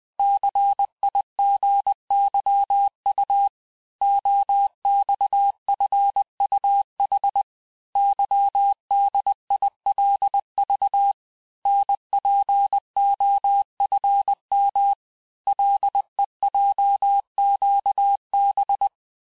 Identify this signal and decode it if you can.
Morse code, heard as a keyed tone, about 20 wpm, 800 Hz. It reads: CIGYU OXFUH YDILV NPOFM LEJQB
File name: cours-1584-teste-chiffre-vitesse-19mpm